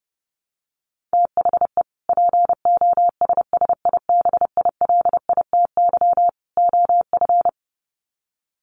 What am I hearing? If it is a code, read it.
Morse code, heard as a keyed tone, about 30 wpm, 700 Hz. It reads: THE POSSIBILITY OF
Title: Sets of 3 Words 30wpm